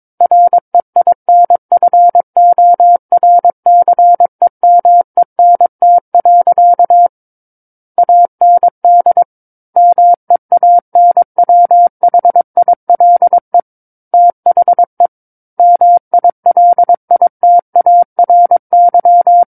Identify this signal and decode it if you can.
Morse code, heard as a keyed tone, about 22 wpm, 700 Hz. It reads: REINFORCEMENT. AND MEANWHILE THE MILITARY
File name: Book One / 13-Chapter 13 - 22 WPM